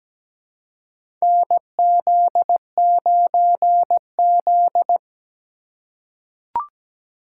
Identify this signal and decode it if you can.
Morse code, heard as a keyed tone, about 17 wpm, 700 Hz. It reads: NZ9Z
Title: USA Call Signs - 4 Characters - 3x Character Spacing - 17wpm